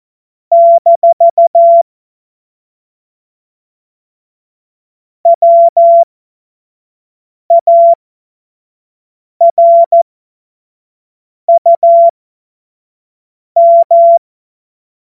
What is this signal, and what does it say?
Morse code, heard as a keyed tone, about 14 wpm, spaced as for 5 wpm, 680 Hz. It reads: - WARUM
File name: Morsekurs / Witze / Witz 18. April 2023 25 F70